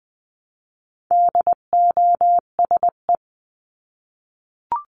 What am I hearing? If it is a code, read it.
Morse code, heard as a keyed tone, about 20 wpm, 700 Hz. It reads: DOSE
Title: 259 Lesson 42d - Words 18-20wpm